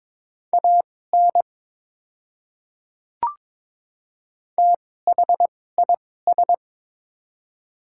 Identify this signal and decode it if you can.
Morse code, heard as a keyed tone, about 22 wpm, spaced as for 16 wpm, 700 Hz. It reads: AN THIS
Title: 2x Character Spacing - Top 300 Words - 22wpm